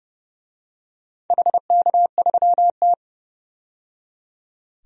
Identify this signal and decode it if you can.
Morse code, heard as a keyed tone, about 30 wpm, 700 Hz. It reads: HK3T
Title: DXCC-100 Call Signs - 4 Characters - 30wpm